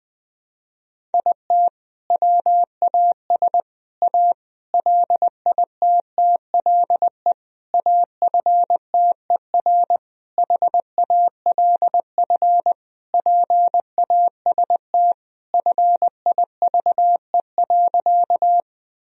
Morse code, heard as a keyed tone, about 21 wpm, 700 Hz. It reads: IT WAS A LITTLE AFTER HALF PAST FIVE.
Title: Sentences from Top 1000 Words - Review Part 3 - 20wpm